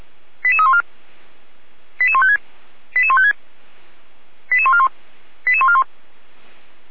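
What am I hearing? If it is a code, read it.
Five-tone selective call (ZVEI), heard as quick runs of five tones, tones of about 70 ms. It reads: 88314 88156 88156 88141 88141
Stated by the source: ZVEI five-tone selective call received on an SDR.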